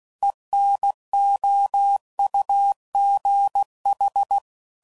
Morse code, heard as a keyed tone, about 16 wpm, 800 Hz. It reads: ENOUGH
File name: CW Code Groups @15WPM / Technical Subjects / XC